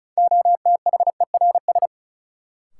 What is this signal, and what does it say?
Morse code, heard as a keyed tone, about 36 wpm, 700 Hz. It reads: OTHERS
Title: Sentences from Top 1000 Words - Review Part 2 - New Word Drill - 35wpm